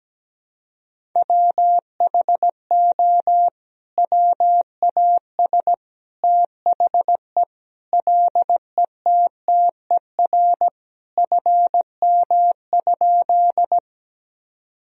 Morse code, heard as a keyed tone, about 17 wpm, 700 Hz. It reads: WHO WAS THE LETTER FM?
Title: Sentences from Top 300 Words and Abbreviations 17wpm